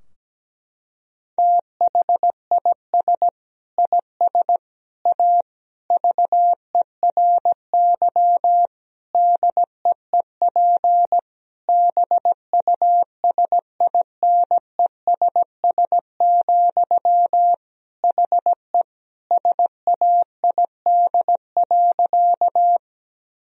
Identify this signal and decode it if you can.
Morse code, heard as a keyed tone, about 17 wpm, 700 Hz. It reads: THIS IS A VERY DEEP BUSINESS, HE SAID.
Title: Sentences from Top 1000 Words - Review Part 3 - 17wpm